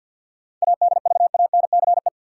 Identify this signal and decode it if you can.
Morse code, heard as a keyed tone, about 53 wpm, 700 Hz. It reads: ADVANCE
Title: Sentences from Top 1400 Words - New Word Drill - 50wpm